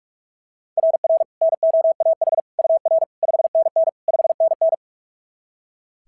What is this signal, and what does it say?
Morse code, heard as a keyed tone, about 43 wpm, 660 Hz. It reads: RR NOAH UR 5NN 5NN
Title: SOTA QSO Practice 45wpm